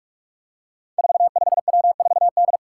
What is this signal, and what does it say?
Morse code, heard as a keyed tone, about 44 wpm, 700 Hz. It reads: 45W4D
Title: Five Challenging Letter-Number Combinations 45wpm